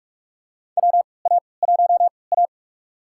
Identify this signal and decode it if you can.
Morse code, heard as a keyed tone, about 46 wpm, spaced as for 26 wpm, 700 Hz. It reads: WA1A
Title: USA Call Signs - 4 Characters - 3x Character Spacing - 45wpm